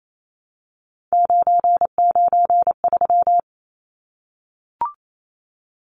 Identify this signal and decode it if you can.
Morse code, heard as a keyed tone, about 28 wpm, 700 Hz. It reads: 993
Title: Numbers - 3-Digit 28wpm